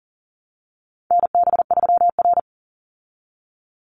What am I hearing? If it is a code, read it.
Morse code, heard as a keyed tone, about 40 wpm, 700 Hz. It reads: NB3R
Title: Call Signs Repeated 3x - 40wpm